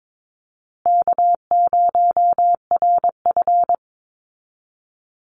Morse code, heard as a keyed tone, about 22 wpm, 700 Hz. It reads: K0RF
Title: Easy Call Signs Repeated 3x - 22wpm